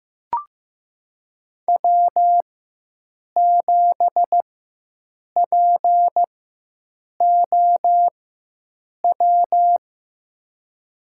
Morse code, heard as a keyed tone, about 15 wpm, spaced as for 7 wpm, 700 Hz. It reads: W7POW
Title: USA Call Signs - 4x Character Spacing - 15wpm